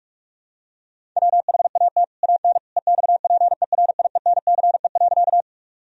Morse code, heard as a keyed tone, about 46 wpm, 700 Hz. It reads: WHAT AN EXPERIENCE.
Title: Sentences from Top 1000 Words - Review Part 3 - 45wpm